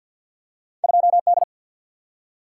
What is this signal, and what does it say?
Morse code, heard as a keyed tone, about 54 wpm, 700 Hz. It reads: 2D
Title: Single-Letter Single-Number 50wpm